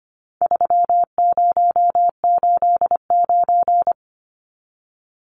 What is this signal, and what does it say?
Morse code, heard as a keyed tone, about 25 wpm, 700 Hz. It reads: 3089
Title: Numbers - 4-Digit 25wpm